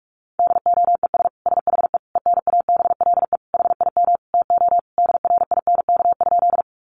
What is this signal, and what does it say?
Morse code, heard as a keyed tone, about 44 wpm, 700 Hz. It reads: DOES SHE ENABLE HIM TO DRINK?
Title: Sentences from Top 1500 Words 45wpm